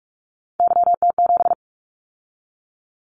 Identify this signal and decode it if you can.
Morse code, heard as a keyed tone, about 44 wpm, 700 Hz. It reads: YT7
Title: Callsign Prefixes for Top 25 DXCC - 45wpm